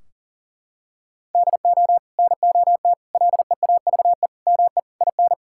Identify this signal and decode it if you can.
Morse code, heard as a keyed tone, about 40 wpm, 700 Hz. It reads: DO NOT LEAVE ME IN
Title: Sentences from Top 1600 Words 40wpm